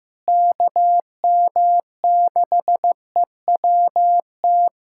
Morse code, heard as a keyed tone, about 15 wpm, 700 Hz. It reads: KM6EWT
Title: USA Callsign DE Callsign - 6 Characters - 15wpm